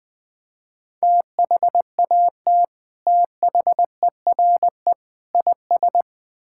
Morse code, heard as a keyed tone, about 20 wpm, 700 Hz. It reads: THAT THERE IS
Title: Build A Phrase - 3-Word Phrases - 20wpm